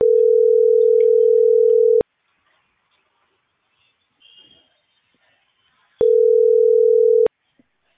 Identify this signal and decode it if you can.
Ringback tone (North America), 440 + 480 Hz.